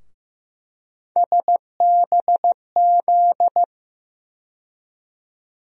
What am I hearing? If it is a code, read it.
Morse code, heard as a keyed tone, about 15 wpm, 700 Hz. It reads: SBZ
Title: USA Call Sign Postfixes - 3 Letters 15wpm